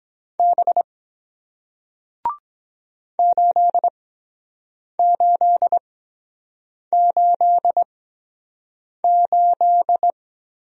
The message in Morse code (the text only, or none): B 8888